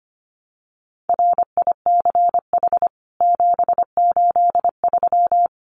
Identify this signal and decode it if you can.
Morse code, heard as a keyed tone, about 25 wpm, 700 Hz. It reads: RICH 783
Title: CWT Exchange 25wpm